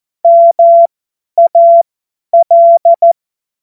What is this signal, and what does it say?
Morse code, heard as a keyed tone, about 14 wpm, spaced as for 10 wpm, 680 Hz. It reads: MAL
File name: Morsekurs / Witze / Witz 15. August 2023 50 F70